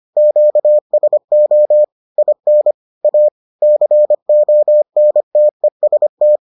Morse code, heard as a keyed tone, about 25 wpm, 600 Hz. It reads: QSO IN A CONTEST